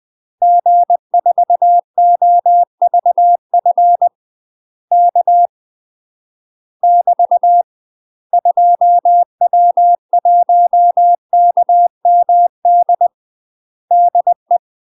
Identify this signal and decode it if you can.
Morse code, heard as a keyed tone, about 20 wpm, 700 Hz. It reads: G4OVF K <BT> 2W1KMD DE
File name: UK MASTER QSOs TRAINING/20WPM